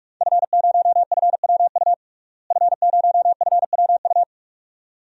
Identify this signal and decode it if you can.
Morse code, heard as a keyed tone, about 41 wpm, 700 Hz. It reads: F0FWU F0FWU
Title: DXCC-100 Call Signs - 5 Characters - Repeated 2x - 45wpm